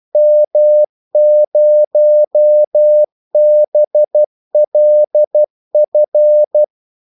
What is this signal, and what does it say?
Morse code, heard as a keyed tone, about 12 wpm, 600 Hz. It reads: M0BLF
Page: 2 ekstra ordmellemrum mellem kaldesignalerne